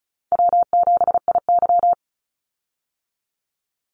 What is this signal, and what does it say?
Morse code, heard as a keyed tone, about 34 wpm, 700 Hz. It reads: W7IY
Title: Call Signs Repeated 2x - 35wpm